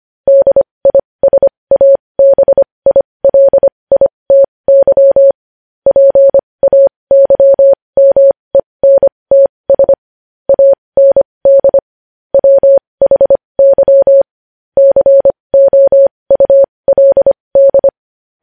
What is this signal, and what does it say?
Morse code, heard as a keyed tone, about 25 wpm, spaced as for 20 wpm, 550 Hz. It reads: DISABILITY PAYMENTS AND WHY COULD
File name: podcast / News headlines-1x-25x20